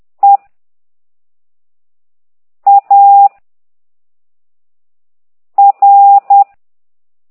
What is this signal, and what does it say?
Morse code, heard as a keyed tone, about 10 wpm, 800 Hz. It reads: E A R
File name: ct2idw / exerc1